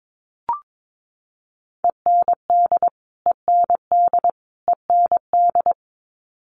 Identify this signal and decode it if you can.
Morse code, heard as a keyed tone, about 22 wpm, 700 Hz. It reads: END END END
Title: Top 200 Words Repeated 22wpm